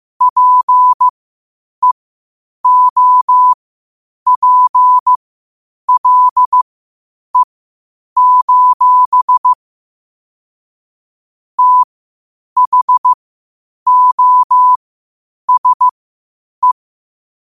Morse code, heard as a keyed tone, about 15 wpm, spaced as for 8 wpm, 1000 Hz. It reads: PEOPLE: THOSE
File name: qotdfiles / 8wpm